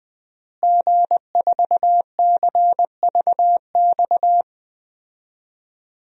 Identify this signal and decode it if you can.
Morse code, heard as a keyed tone, about 20 wpm, 700 Hz. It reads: G4CVX
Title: DXCC-100 Call Signs - 5 Characters - Repeated 2x - 20wpm